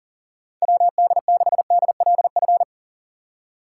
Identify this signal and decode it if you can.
Morse code, heard as a keyed tone, about 40 wpm, 700 Hz. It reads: WD6DLF
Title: USA Call Signs - 6 Characters - 6x Character Spacing - 40wpm